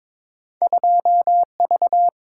Morse code, heard as a keyed tone, about 22 wpm, 700 Hz. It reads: 2V
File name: practice / 22wpm / Two Letter-Number Combinations - 22wpm